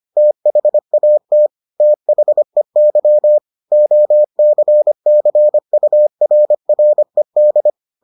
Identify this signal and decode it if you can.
Morse code, heard as a keyed tone, about 25 wpm, 600 Hz. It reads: THAT THEY OCCURRED